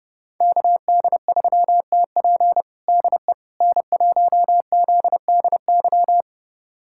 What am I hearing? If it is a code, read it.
Morse code, heard as a keyed tone, about 30 wpm, 700 Hz. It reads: KD3TP DE N1ZDY
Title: USA Callsign DE Callsign - 5 Characters - 30wpm